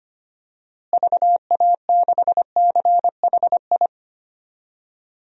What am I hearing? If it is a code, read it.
Morse code, heard as a keyed tone, about 25 wpm, 700 Hz. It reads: VA6CHI
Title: DXCC-100 Call Signs - 6 Characters - Repeated 2x - 25wpm